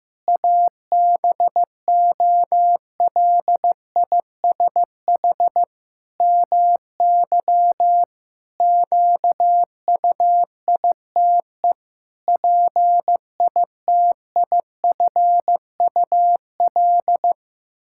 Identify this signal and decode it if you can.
Morse code, heard as a keyed tone, about 15 wpm, 700 Hz. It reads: ABOLISH MY QUITE PITIFUL
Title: Pangram Challenge v2 - 15wpm